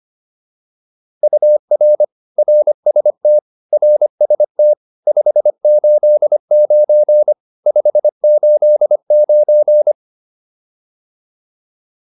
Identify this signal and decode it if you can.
Morse code, heard as a keyed tone, about 25 wpm, 600 Hz. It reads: UR RST RST 589 589
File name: QSO409 25